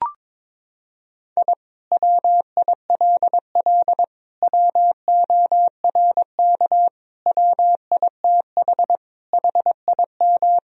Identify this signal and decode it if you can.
Morse code, heard as a keyed tone, about 22 wpm, 700 Hz. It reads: I WILL WORK WITH HIM
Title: Sentences from Top 1000 Words - Review Part 4 - 22wpm